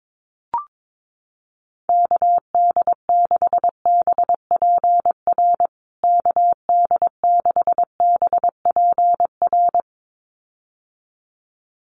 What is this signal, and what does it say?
Morse code, heard as a keyed tone, about 22 wpm, 700 Hz. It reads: KD6BPR KD6BPR
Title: USA Call Signs Repeated 2x 22wpm